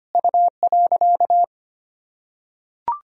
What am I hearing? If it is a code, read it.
Morse code, heard as a keyed tone, about 25 wpm, 700 Hz. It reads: U.
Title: Sentences from Top 200 Words and Abbreviations 25wpm